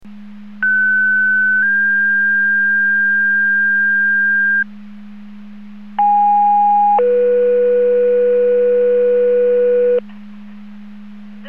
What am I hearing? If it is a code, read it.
Two-tone page, heard as a one-second tone, then a three-second tone, about 1500 then 1600 Hz, about 820 then 490 Hz. It is heard from 0.6 to 10.0 s.